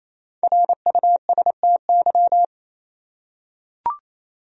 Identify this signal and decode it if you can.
Morse code, heard as a keyed tone, about 28 wpm, 700 Hz. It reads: RUSTY
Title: USA Common Names - Max 5 Letters - 28wpm